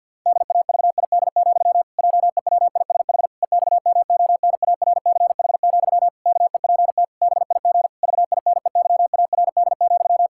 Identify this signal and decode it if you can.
Morse code, heard as a keyed tone, about 50 wpm, 700 Hz. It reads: DAVID, JEWISH EXMONARCH, KEPT BIG VINEYARD,
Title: Pangram Challenge v2 - 50wpm